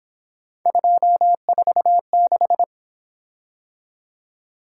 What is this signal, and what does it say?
Morse code, heard as a keyed tone, about 27 wpm, 700 Hz. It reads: 246
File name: Speed-Racing - Three Letter-Number Combinations - 22wpm